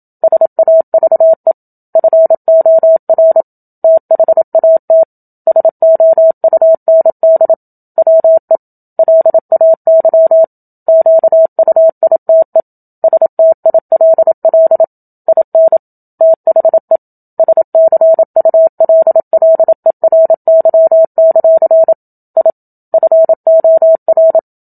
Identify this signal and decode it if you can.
Morse code, heard as a keyed tone, about 27 wpm, 650 Hz. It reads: SAVE FOR THAT SOUND WE LAY QUITE STILL IN THE SCULLERY; I FOR
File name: Book Two / 01-Chapter 1 - 27 WPM